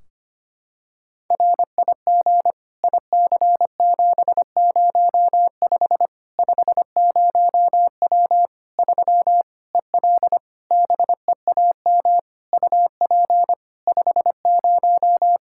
Morse code, heard as a keyed tone, about 25 wpm, 700 Hz. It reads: RIG IC705 50W 3 EL BEAM UP 50
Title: Practice QSO - Exchange 2 - Info - 25wpm